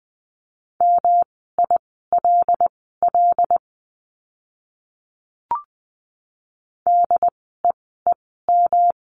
Morse code, heard as a keyed tone, about 20 wpm, spaced as for 14 wpm, 700 Hz. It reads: MILL DEEM
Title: 108 Lesson 18f - Letter m - Words 20wpm